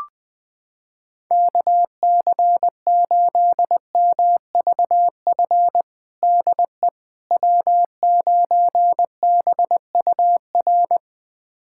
Morse code, heard as a keyed tone, about 20 wpm, 700 Hz. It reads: KC8MVF DE W9BUR
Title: USA Callsign DE Callsign 20wpm